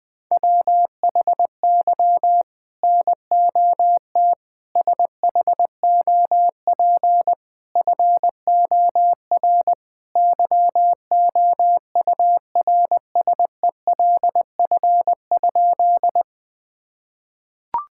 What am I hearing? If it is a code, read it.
Morse code, heard as a keyed tone, about 20 wpm, 700 Hz. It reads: WHY NOT SHOP FOR YOURSELF?
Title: Sentences from Top 1600 Words 20wpm